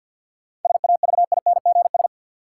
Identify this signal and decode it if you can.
Morse code, heard as a keyed tone, about 48 wpm, 700 Hz. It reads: SAVINGS
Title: Sentences from Top 1200 Words - New Word Drill Alt - 50wpm